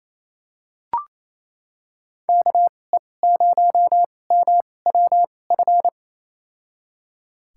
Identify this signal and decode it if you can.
Morse code, heard as a keyed tone, about 28 wpm, spaced as for 20 wpm, 700 Hz. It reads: KE0MWF
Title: USA Call Signs - 2x Character Spacing - 28wpm